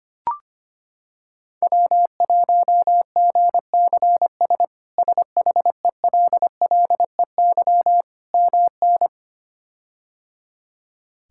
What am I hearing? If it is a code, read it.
Morse code, heard as a keyed tone, about 25 wpm, 700 Hz. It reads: W1GCS SHELLEY MN
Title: K1USN SST QSO Practice - 25wpm